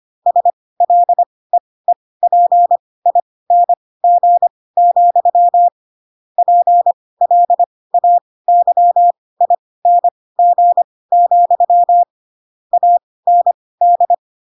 Morse code, heard as a keyed tone, about 25 wpm, spaced as for 18 wpm, 700 Hz. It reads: SLEEPING, PLAYING, AND
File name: Pinocchio / Pinocchio ch4 12 25 ik1hge